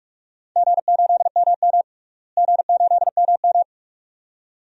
Morse code, heard as a keyed tone, about 44 wpm, 700 Hz. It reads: G8MM G8MM
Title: DXCC-100 Call Signs - 4 Characters - Repeated 2x - 45wpm